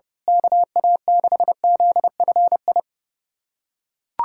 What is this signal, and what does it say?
Morse code, heard as a keyed tone, about 30 wpm, 700 Hz. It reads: KA6ZFI
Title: USA Callsign DE Callsign - 6 Characters - 30wpm